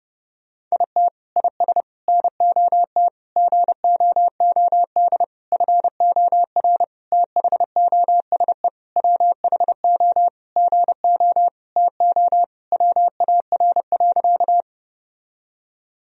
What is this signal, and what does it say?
Morse code, heard as a keyed tone, about 32 wpm, 700 Hz. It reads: IT IS NOT GOOD FOR THOSE WHO GO TO WAR.
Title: Sentences from Top 1000 Words - Review Part 2 - 30wpm